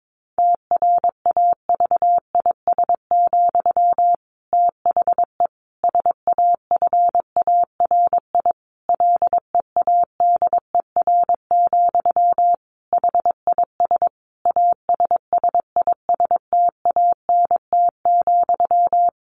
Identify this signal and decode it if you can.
Morse code, heard as a keyed tone, about 22 wpm, 700 Hz. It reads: TRAVIS, THE SAFARI LEADER, HIS ASSISTANT,
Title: A Sound Of Thunder 22wpm